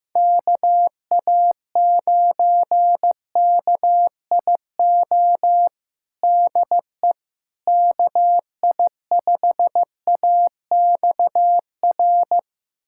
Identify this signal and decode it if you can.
Morse code, heard as a keyed tone, about 15 wpm, 700 Hz. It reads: KA9KIO DE KI5AXR
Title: USA Callsign DE Callsign - 6 Characters - 15wpm